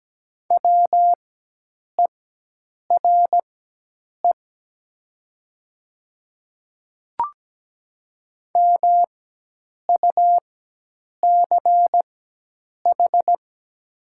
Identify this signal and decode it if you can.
Morse code, heard as a keyed tone, about 17 wpm, spaced as for 8 wpm, 700 Hz. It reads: WERE MUCH
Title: 4x Character Spacing - Top 200 Words - 17wpm